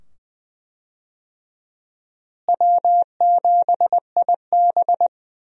Morse code, heard as a keyed tone, about 20 wpm, 700 Hz. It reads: W7IB
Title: USA Call Signs - 4 Characters - Repeated 2x - 20wpm